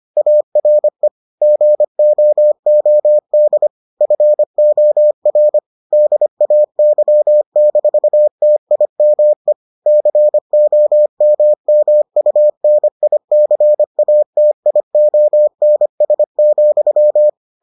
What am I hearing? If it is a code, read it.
Morse code, heard as a keyed tone, about 25 wpm, 600 Hz. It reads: ARE GOOD FOR DAY-TIME COMMUNICATIONS,